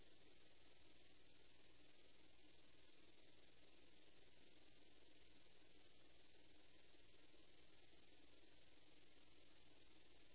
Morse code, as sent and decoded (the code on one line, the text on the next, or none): none